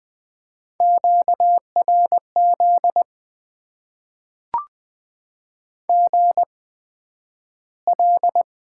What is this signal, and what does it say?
Morse code, heard as a keyed tone, about 20 wpm, spaced as for 11 wpm, 700 Hz. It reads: QRZ GL
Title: Top 90 QSO Elements - 8x Character Spacing - 20wpm